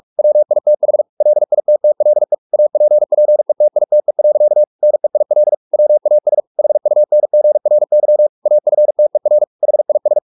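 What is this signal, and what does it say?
Morse code, heard as a keyed tone, about 44 wpm, 600 Hz. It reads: WITH LITTLE APPETITE. NEIL WAS HUNGRY AFTER HIS
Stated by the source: Morse practice at 45 wpm, the novel Never by Ken Follett in English, from CW med Gnister.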